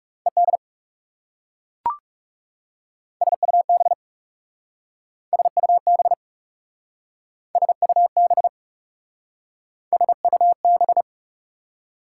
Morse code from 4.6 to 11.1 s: SUB SUB SUB

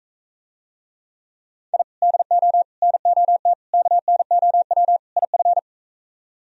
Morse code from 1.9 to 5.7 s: DO NOT KNOW IF

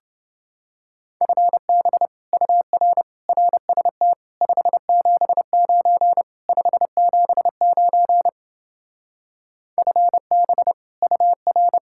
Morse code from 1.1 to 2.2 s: FB

